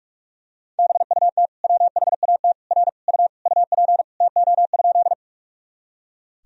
Morse code, heard as a keyed tone, about 42 wpm, 700 Hz. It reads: BUT WHAT R U UP TO?
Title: Sentences from Top 100 Words and Abbreviations 45wpm